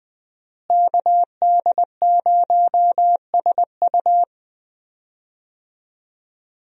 Morse code, heard as a keyed tone, about 20 wpm, 700 Hz. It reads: KD0SU
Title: USA Call Signs - 5 Characters - Repeated 2x - 20wpm